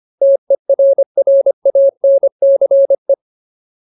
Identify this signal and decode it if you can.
Morse code, heard as a keyed tone, about 25 wpm, 550 Hz. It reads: TERRANCE
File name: Ultimate Word List / 25WPM Ultimate